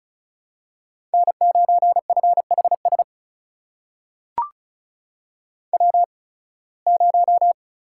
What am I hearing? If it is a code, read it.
Morse code, heard as a keyed tone, about 36 wpm, spaced as for 23 wpm, 700 Hz. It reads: N9FHS W0
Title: USA Call Signs - 5 Characters - 8x Character Spacing - 35wpm